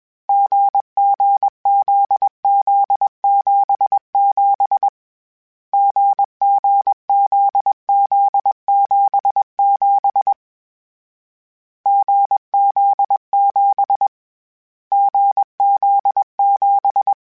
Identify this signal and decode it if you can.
Morse code, heard as a keyed tone, about 21 wpm, 800 Hz. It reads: GGZZ77 GGZZ77 GZ7 GZ7
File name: Difficult Combos / Difficult Combos-mp3s / GZ GZ7 @20